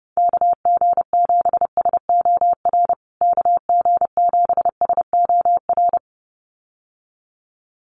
Morse code, heard as a keyed tone, about 30 wpm, 700 Hz. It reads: KG7SOR KG7SOR
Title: USA Call Signs Repeated 2x 30wpm